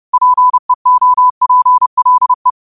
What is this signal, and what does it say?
Morse code, heard as a keyed tone, about 30 wpm, 1000 Hz. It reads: PEOPLE
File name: qotdfiles / 30wpm